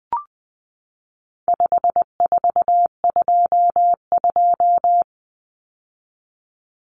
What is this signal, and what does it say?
Morse code, heard as a keyed tone, about 20 wpm, 700 Hz. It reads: 5422
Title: Numbers - 4-Digit 20wpm